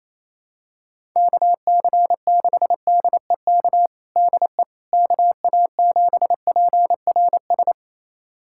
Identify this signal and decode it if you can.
Morse code, heard as a keyed tone, about 28 wpm, 700 Hz. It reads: KC6DEK DE KA7PRS
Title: USA Callsign DE Callsign - 6 Characters - 28wpm